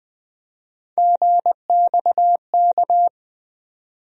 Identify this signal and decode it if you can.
Morse code, heard as a keyed tone, about 20 wpm, 700 Hz. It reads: GXK
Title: USA Call Sign Postfixes - 3 Letters 20wpm